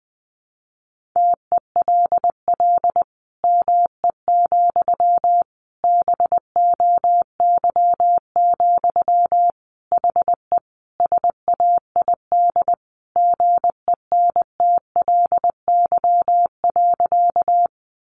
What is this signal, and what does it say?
Morse code, heard as a keyed tone, about 20 wpm, 700 Hz. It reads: TELL ME, BOY, HE SAID GENTLY.
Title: Sentences from Top 1500 Words 20wpm